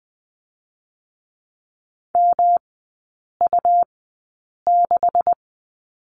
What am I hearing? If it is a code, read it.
Morse code, heard as a keyed tone, about 20 wpm, spaced as for 8 wpm, 700 Hz. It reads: MU6
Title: 127 Lesson 21e - Number 6 - Three Characters 20wpm